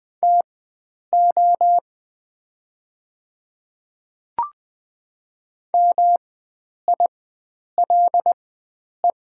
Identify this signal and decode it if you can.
Morse code, heard as a keyed tone, about 20 wpm, spaced as for 9 wpm, 700 Hz. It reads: TO MILE
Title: 4x Character Spacing - Top 300 Words - 20wpm